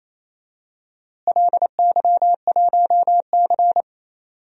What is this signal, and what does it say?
Morse code, heard as a keyed tone, about 28 wpm, 700 Hz. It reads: LY1C
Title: DXCC-100 Call Signs - 4 Characters - Repeated 3x - 28wpm